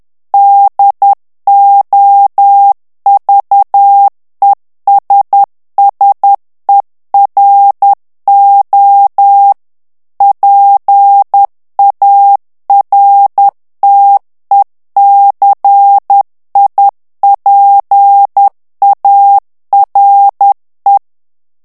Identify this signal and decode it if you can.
Morse code, heard as a keyed tone, about 10 wpm, 800 Hz. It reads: DOVESSERO PARTECIPARE
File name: ITALIEN texte clair 2 vitesse 10 mots minute